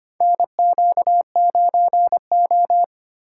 Morse code, heard as a keyed tone, about 25 wpm, 700 Hz. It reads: NQ9O
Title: USA Call Signs - 4 Characters - 4x Character Spacing - 25wpm